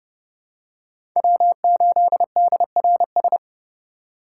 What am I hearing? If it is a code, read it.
Morse code, heard as a keyed tone, about 30 wpm, 700 Hz. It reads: W8DRS
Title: USA Call Signs - 5 Characters - 3x Character Spacing - 30wpm